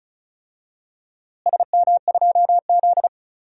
Morse code, heard as a keyed tone, about 34 wpm, 700 Hz. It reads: SM2Z
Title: DXCC-100 Call Signs - 4 Characters - Repeated 3x - 35wpm